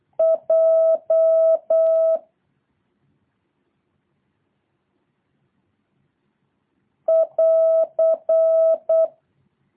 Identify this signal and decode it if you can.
Morse code, heard as a keyed tone, about 8 wpm, 650 Hz. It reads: J <AR>
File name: jh8cbh / 2023cw